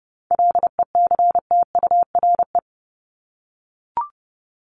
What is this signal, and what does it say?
Morse code, heard as a keyed tone, about 30 wpm, 700 Hz. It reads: LECTURE
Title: Top 1501-1750 Words 30wpm